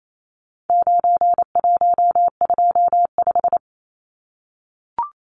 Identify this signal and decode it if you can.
Morse code, heard as a keyed tone, about 28 wpm, 700 Hz. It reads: 9125
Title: Numbers - 4-Digit 28wpm